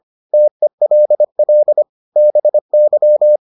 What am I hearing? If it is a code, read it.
Morse code, heard as a keyed tone, about 25 wpm, 600 Hz. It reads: TELL BY